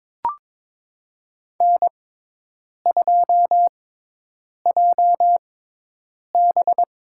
Morse code, heard as a keyed tone, about 21 wpm, spaced as for 8 wpm, 700 Hz. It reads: N2JB
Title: USA Call Signs - 4 Characters - 6x Character Spacing - 22wpm